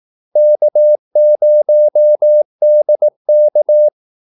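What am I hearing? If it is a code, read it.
Morse code, heard as a keyed tone, about 18 wpm, 600 Hz. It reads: K0DK
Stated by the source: Morse call-sign practice at 18 wpm.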